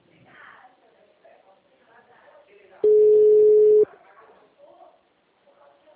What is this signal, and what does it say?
Ringback tone (Europe, 425 Hz), with speech.
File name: out- -2002- - - .27635